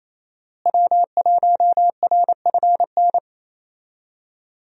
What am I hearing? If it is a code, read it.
Morse code, heard as a keyed tone, about 28 wpm, 700 Hz. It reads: W1RFN
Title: USA Call Signs - 5 Characters - Repeated 3x - 28wpm